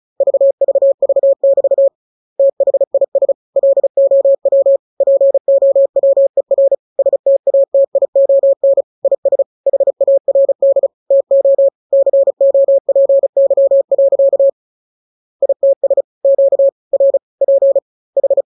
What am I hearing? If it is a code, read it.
Morse code, heard as a keyed tone, about 36 wpm, 550 Hz. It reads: VVV<BT> THIS LOW POWER STATION IS HARD TO COPY. ITS QRP H
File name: jokes / joke 4 35 0000